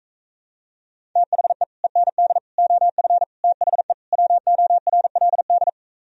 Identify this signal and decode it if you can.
Morse code, heard as a keyed tone, about 45 wpm, 700 Hz. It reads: THE END OF THE WORLD